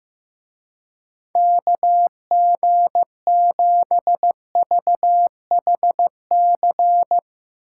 Morse code, heard as a keyed tone, about 15 wpm, 700 Hz. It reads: KG7VHC
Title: Speed Racing - USA 2x3 Call Signs 15wpm